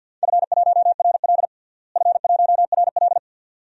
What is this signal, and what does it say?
Morse code, heard as a keyed tone, about 47 wpm, 700 Hz. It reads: F1RL F1RL
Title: DXCC-100 Call Signs - 4 Characters - Repeated 2x - 50wpm